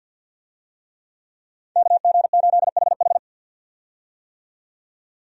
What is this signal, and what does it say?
Morse code, heard as a keyed tone, about 51 wpm, 700 Hz. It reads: KG8HH
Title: USA Call Signs - 5 Characters - 50wpm